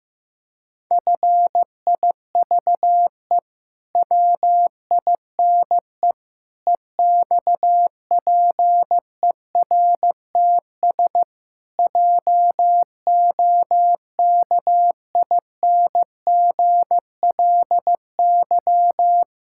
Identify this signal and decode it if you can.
Morse code, heard as a keyed tone, about 15 wpm, 700 Hz. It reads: FIVE WINE EXPERTS JOKINGLY
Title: Pangram Challenge v2 - 15wpm